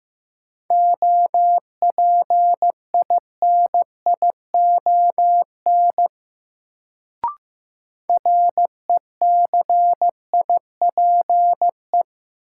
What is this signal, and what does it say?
Morse code, heard as a keyed tone, about 15 wpm, 700 Hz. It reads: OPINION RECIPE
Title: Sentences from Top 1200 Words - New Word Drill - 15wpm